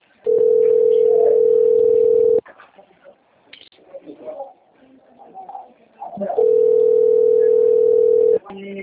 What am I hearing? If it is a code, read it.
Ringback tone (North America), 440 + 480 Hz, with speech.
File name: rg-1600-unknown- - - .85986